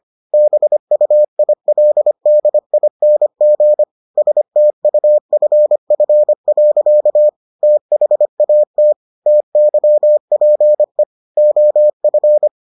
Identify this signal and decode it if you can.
Morse code, heard as a keyed tone, about 25 wpm, 600 Hz. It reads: BUILDING STUFF. THAT TYPE OF